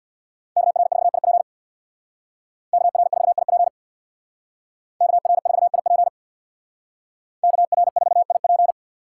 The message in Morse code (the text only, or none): KR4IP KR4IP KR4IP KR4IP